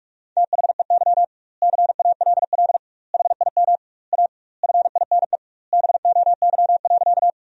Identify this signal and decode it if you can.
Morse code, heard as a keyed tone, about 46 wpm, 700 Hz. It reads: THEY CALL HIM A FINE BOY.
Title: 2x Word-Spacing - Sentences from Top 500 Words - 45wpm